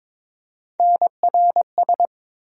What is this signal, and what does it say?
Morse code, heard as a keyed tone, about 21 wpm, 700 Hz. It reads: NRS
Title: USA Call Sign Postfixes - 3 Letters 22wpm